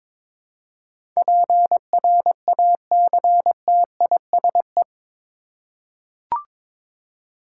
Morse code, heard as a keyed tone, about 22 wpm, 700 Hz. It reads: PRACTISE
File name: practice / 22wpm / Eight Letter Words - 22wpm